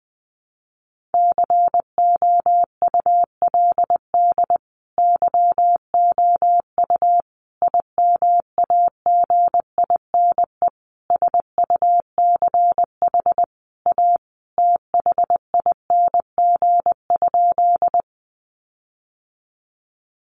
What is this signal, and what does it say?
Morse code, heard as a keyed tone, about 20 wpm, 700 Hz. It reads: COULD YOU IMAGINE SUCH A THING?
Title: Sentences from Top 1400 Words 20wpm